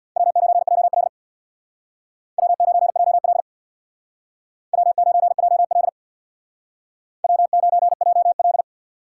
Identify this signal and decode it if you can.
Morse code, heard as a keyed tone, about 63 wpm, 700 Hz. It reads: W9JL W9JL W9JL W9JL